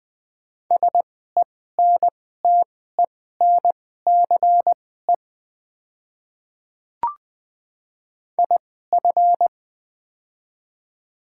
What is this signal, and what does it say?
Morse code, heard as a keyed tone, about 20 wpm, spaced as for 15 wpm, 700 Hz. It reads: SENTENCE IF
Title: 2x Character Spacing - Top 200 Words - 20wpm